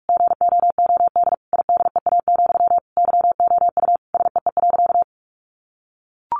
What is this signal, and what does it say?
Morse code, heard as a keyed tone, about 44 wpm, 700 Hz. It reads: GOOD IDEA, YOU SEE.
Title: Sentences from Top 600 Words 45wpm